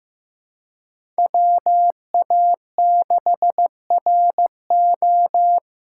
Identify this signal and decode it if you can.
Morse code, heard as a keyed tone, about 15 wpm, 700 Hz. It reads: WA6RO
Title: USA Call Signs - 3x Character Spacing - 15wpm